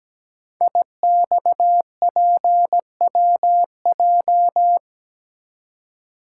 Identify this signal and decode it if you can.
Morse code, heard as a keyed tone, about 17 wpm, 700 Hz. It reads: IXPWJ
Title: Code Groups - 5 Letters - 17wpm